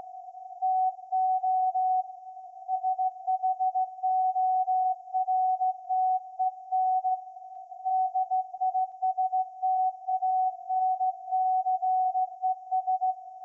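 Morse code, heard as a keyed tone, about 16 wpm, 750 Hz. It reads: TO SHORTEN DISTANCES
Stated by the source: SAQ VLF Morse transmission received by IZ7SLZ, Putignano (Apulia), Italy.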